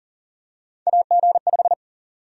Morse code, heard as a keyed tone, about 40 wpm, 700 Hz. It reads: AG5
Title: USA Call Sign Prefixes - Repeated 2x - 40wpm